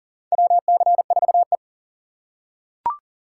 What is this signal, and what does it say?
Morse code, heard as a keyed tone, about 37 wpm, 700 Hz. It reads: WC4E
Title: USA Call Signs - 4 Characters - 4x Character Spacing - 40wpm